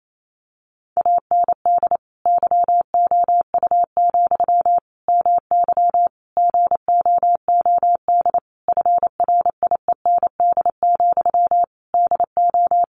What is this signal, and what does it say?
Morse code, heard as a keyed tone, about 28 wpm, 700 Hz. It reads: AND YOU, MY GOOD FRIEND, DO
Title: Sentences from Top 1100 Words 28wpm